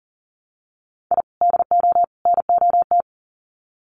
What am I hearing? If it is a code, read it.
Morse code, heard as a keyed tone, about 40 wpm, 700 Hz. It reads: I DO NOT
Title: Sets of 3 Words 40wpm